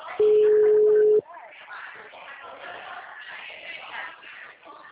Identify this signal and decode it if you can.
Ringback tone (Europe, 425 Hz), with speech.